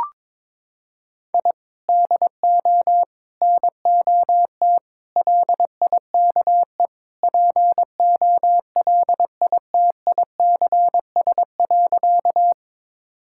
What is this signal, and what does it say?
Morse code, heard as a keyed tone, about 22 wpm, 700 Hz. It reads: I DO NOT LIKE POLITICS.
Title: Sentences from Top 1000 Words - Review Part 6 - 22wpm